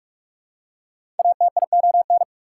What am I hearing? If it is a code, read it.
Morse code, heard as a keyed tone, about 42 wpm, 700 Hz. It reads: ATION
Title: Prefixes and Suffixes 45wpm